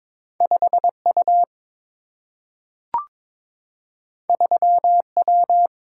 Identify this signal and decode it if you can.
Morse code, heard as a keyed tone, about 22 wpm, 700 Hz. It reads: 5U 3W
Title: Single-Letter Single-Number 22wpm